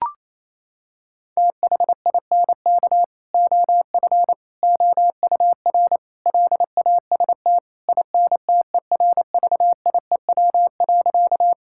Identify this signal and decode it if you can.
Morse code, heard as a keyed tone, about 28 wpm, 700 Hz. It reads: THINK OF OUR LAST INTERVIEW.
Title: Sentences from Top 1600 Words 28wpm